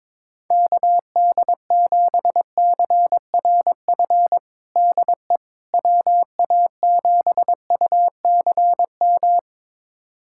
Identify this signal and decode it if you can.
Morse code, heard as a keyed tone, about 22 wpm, 700 Hz. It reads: KD7CRF DE WA7UCM
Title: USA Callsign DE Callsign - 6 Characters - 22wpm